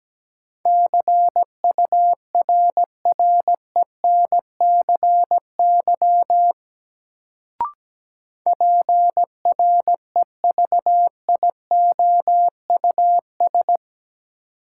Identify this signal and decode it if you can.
Morse code, heard as a keyed tone, about 17 wpm, 700 Hz. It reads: CURRENCY PREVIOUS
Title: Sentences from Top 1500 Words - New Word Drill Alt - 17wpm